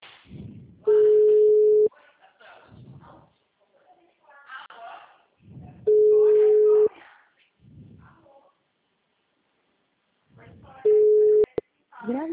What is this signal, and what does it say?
Ringback tone (Europe, 425 Hz), with speech.